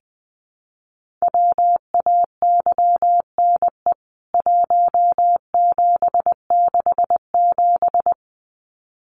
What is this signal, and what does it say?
Morse code, heard as a keyed tone, about 20 wpm, 700 Hz. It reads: WAYNE 1767
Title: CWT Exchange 20wpm